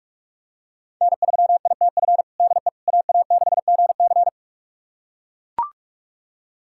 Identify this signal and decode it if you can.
Morse code, heard as a keyed tone, about 41 wpm, 700 Hz. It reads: N3ITF DE AA6GC
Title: USA Callsign DE Callsign - 5 Characters - 45wpm